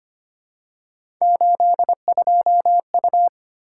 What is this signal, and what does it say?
Morse code, heard as a keyed tone, about 25 wpm, 700 Hz. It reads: 82U
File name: Three Letter-Number Combinations - 25wpm